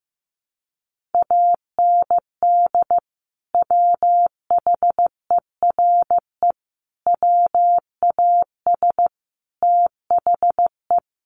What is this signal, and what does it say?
Morse code, heard as a keyed tone, about 15 wpm, 700 Hz. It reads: AND WHERE WAS THE
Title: Sentences from Top 1500 Words 15wpm